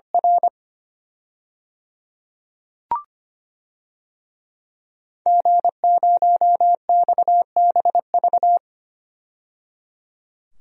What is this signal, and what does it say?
Morse code, heard as a keyed tone, about 25 wpm, 700 Hz. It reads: R G0XBV
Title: DXCC-100 Call Signs - 5 Characters - 25wpm